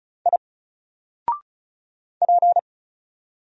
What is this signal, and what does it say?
Morse code, heard as a keyed tone, about 36 wpm, 700 Hz. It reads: I P
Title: Single Character (Letter:Number:Punctuation:Prosign) 35wpm